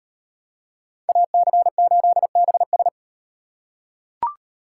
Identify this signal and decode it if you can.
Morse code, heard as a keyed tone, about 38 wpm, 700 Hz. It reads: AC8BS